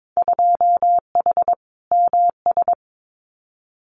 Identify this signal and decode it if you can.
Morse code, heard as a keyed tone, about 22 wpm, 700 Hz. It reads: 2H MS
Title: Winter Field Day Exchanges 22wpm